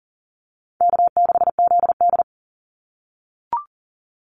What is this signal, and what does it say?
Morse code, heard as a keyed tone, about 40 wpm, 700 Hz. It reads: K6ZD
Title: Call Signs - 40wpm